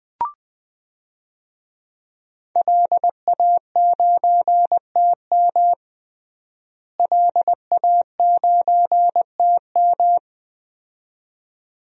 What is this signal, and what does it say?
Morse code, heard as a keyed tone, about 20 wpm, 700 Hz. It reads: LA9TM LA9TM
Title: DXCC-100 Call Signs - 5 Characters - Repeated 2x - 20wpm